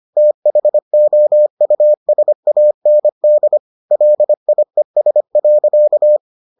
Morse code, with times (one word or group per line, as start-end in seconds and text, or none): 0.2-3.6 THOUSAND
3.9-6.2 LIES.